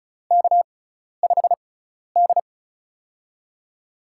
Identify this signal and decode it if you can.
Morse code, heard as a keyed tone, about 36 wpm, spaced as for 12 wpm, 700 Hz. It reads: K5D
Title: USA Call Signs - 4 Characters - 6x Character Spacing - 35wpm